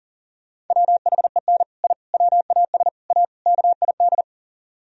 Morse code, heard as a keyed tone, about 37 wpm, 700 Hz. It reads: WHEN I WAS A KID
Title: Speed-Racing - Sets of 5 Words 30wpm